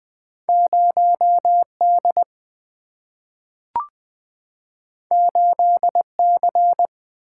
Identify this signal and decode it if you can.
Morse code, heard as a keyed tone, about 20 wpm, 700 Hz. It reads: 0D 8C
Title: Two Letter-Number Combinations - 20wpm